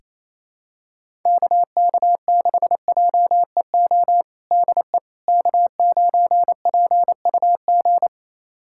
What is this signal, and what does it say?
Morse code, heard as a keyed tone, about 28 wpm, 700 Hz. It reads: KK6JEO DE K9PUG
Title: USA Callsign DE Callsign 28wpm